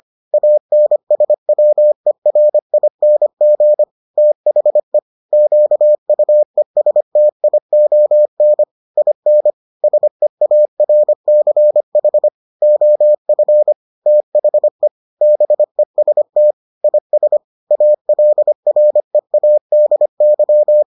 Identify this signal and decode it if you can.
Morse code, heard as a keyed tone, about 25 wpm, 600 Hz. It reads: ANSWERING THE QUESTION IN SEARCH OF THE BEST IS ALREADY